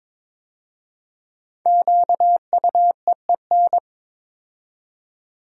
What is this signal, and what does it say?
Morse code, heard as a keyed tone, about 22 wpm, 700 Hz. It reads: QUEEN
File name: 5 Letter Words - First 3 Letters Given - 22wpm